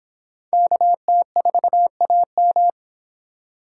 Morse code, heard as a keyed tone, about 24 wpm, 700 Hz. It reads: KT4AM